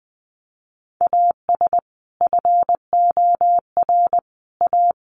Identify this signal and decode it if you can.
Morse code, heard as a keyed tone, about 20 wpm, 700 Hz. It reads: AS FOR A
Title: Sentences from Top 1400 Words 20wpm